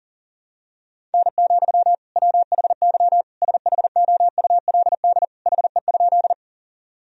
Morse code, heard as a keyed tone, about 37 wpm, 700 Hz. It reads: N, WHY SHOULD HE?
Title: Sentences from Top 300 Words and Abbreviations 40wpm